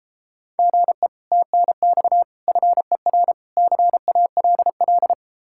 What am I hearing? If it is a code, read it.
Morse code, heard as a keyed tone, about 31 wpm, 700 Hz. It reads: GE TNX FER CALL